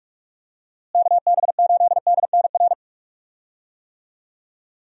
Morse code, heard as a keyed tone, about 42 wpm, 700 Hz. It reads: KB8DNR
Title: USA Call Signs - 6 Characters - Repeated 2x - 45wpm